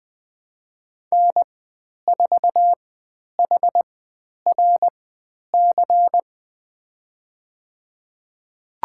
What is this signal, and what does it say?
Morse code, heard as a keyed tone, about 20 wpm, spaced as for 10 wpm, 700 Hz. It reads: N4HRC
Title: 095 Lesson 16g - Letter c - Call Signs 20wpm